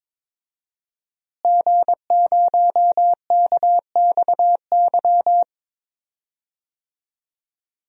Morse code, heard as a keyed tone, about 22 wpm, 700 Hz. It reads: G0KXY
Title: DXCC-100 Call Signs - 5 Characters - Repeated 2x - 22wpm